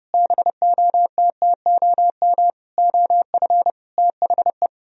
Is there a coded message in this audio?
Yes: Morse code from 0.1 to 4.7 s, heard as a keyed tone, about 32 wpm, 700 Hz. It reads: BOTTOM OF THE